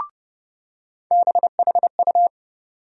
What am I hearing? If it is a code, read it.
Morse code, heard as a keyed tone, about 30 wpm, 700 Hz. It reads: BHU
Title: USA Call Sign Postfixes - 3 Letters 30wpm